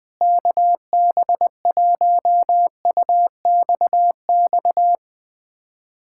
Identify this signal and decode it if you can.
Morse code, heard as a keyed tone, about 20 wpm, 700 Hz. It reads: KB1UXX
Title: USA Callsign DE Callsign - 6 Characters - 20wpm